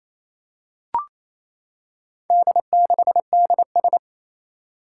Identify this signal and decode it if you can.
Morse code, heard as a keyed tone, about 29 wpm, 700 Hz. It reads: D6DS
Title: Four Challenging Letter-Number Combinations 28wpm